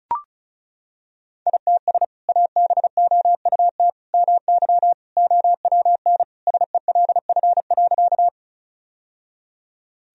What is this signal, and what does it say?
Morse code, heard as a keyed tone, about 36 wpm, 700 Hz. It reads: ITS ABOUT MY OWN SELF.
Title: Sentences from Top 1000 Words - Review Part 1 - 35wpm